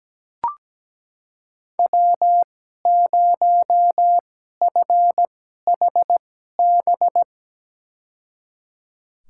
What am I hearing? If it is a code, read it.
Morse code, heard as a keyed tone, about 17 wpm, spaced as for 12 wpm, 700 Hz. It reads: W0FHB
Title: USA Call Signs - 5 Characters - 2x Character Spacing - 17wpm